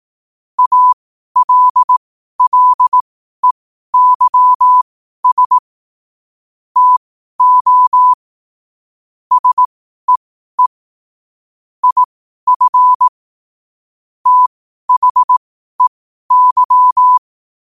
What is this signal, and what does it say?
Morse code, heard as a keyed tone, about 18 wpm, spaced as for 12 wpm, 1000 Hz. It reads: ALLEYS TO SEE IF THEY
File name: qotdfiles / 12wpm